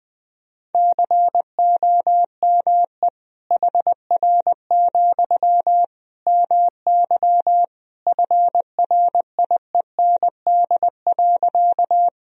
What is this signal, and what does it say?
Morse code, heard as a keyed tone, about 20 wpm, 700 Hz. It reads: COME HR, MY FRIEND.
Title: Sentences from Top 300 Words and Abbreviations 20wpm